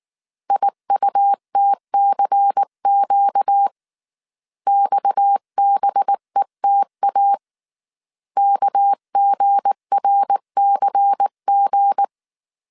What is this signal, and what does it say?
Morse code, heard as a keyed tone, about 19 wpm, 800 Hz. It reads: IUTCQ XBETA KGRCG
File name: CW / DET / 1 9 1500C S B 18wpm 2551